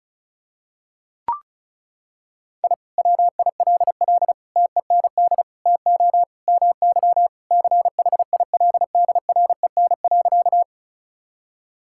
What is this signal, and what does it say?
Morse code, heard as a keyed tone, about 34 wpm, 700 Hz. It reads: I WILL TEND TO MY CHILDREN.
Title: Sentences from Top 1000 Words - Review Part 5 - 35wpm